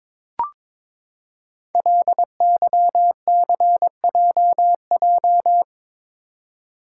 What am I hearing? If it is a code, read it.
Morse code, heard as a keyed tone, about 22 wpm, 700 Hz. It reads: LYCJJ
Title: Code Groups - 5 Letter:Numbers - 22wpm